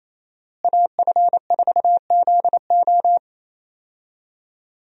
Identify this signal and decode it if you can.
Morse code, heard as a keyed tone, about 27 wpm, 700 Hz. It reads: AF4ZO
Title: USA Call Signs - 5 Characters - Repeated 3x - 28wpm